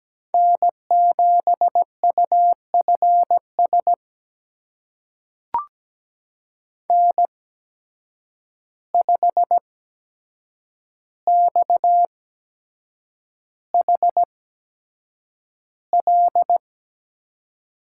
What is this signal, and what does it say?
Morse code, heard as a keyed tone, about 17 wpm, spaced as for 7 wpm, 700 Hz. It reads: N7UFS N5XHL
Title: USA Call Signs - 5 Characters - 8x Character Spacing - 17wpm